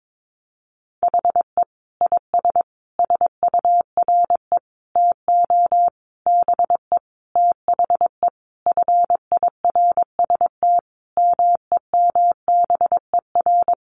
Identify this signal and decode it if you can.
Morse code, heard as a keyed tone, about 22 wpm, 700 Hz. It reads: HE IS SURE TO BE THE FIRST MEMBER
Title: Sentences from Top 1400 Words 22wpm